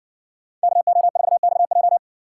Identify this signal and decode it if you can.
Morse code, heard as a keyed tone, about 55 wpm, 700 Hz. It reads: KO4XJ